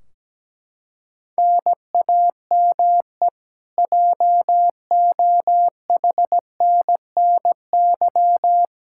Morse code, heard as a keyed tone, about 17 wpm, 700 Hz. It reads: NAME JOHNNY
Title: Practice QSO - Exchange 1 - Beginning - 17wpm